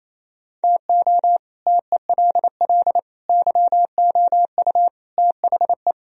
Morse code, heard as a keyed tone, about 27 wpm, 700 Hz. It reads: TO TELL YOU THE
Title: Speed-Racing - Sets of 5 Words 28wpm